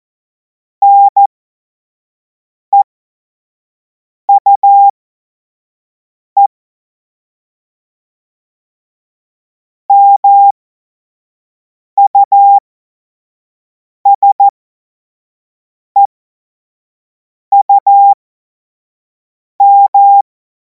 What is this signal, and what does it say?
Morse code, heard as a keyed tone, about 14 wpm, spaced as for 5 wpm, 800 Hz. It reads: NEUE MUSEUM
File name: Morsekurs / Witze / Witz 31. März 2026 25 F70